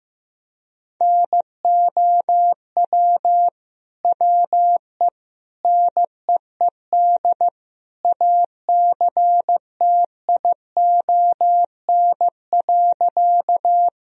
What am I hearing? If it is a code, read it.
Morse code, heard as a keyed tone, about 15 wpm, 700 Hz. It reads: NOW WE NEED ACTION.
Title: Sentences from Top 1000 Words - Review Part 4 - 15wpm